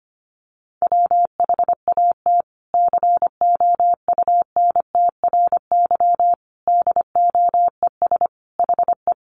Morse code, heard as a keyed tone, about 25 wpm, 700 Hz. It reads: WHAT COUNTRY DOES HE
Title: Sentences from Top 300 Words 25wpm